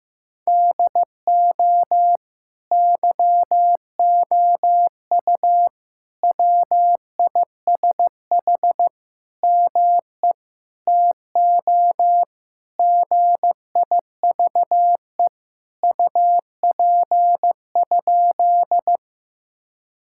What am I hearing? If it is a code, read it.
Morse code, heard as a keyed tone, about 15 wpm, 700 Hz. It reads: DO YOU WISH ME TO GIVE UP?
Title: Sentences from Top 1600 Words 15wpm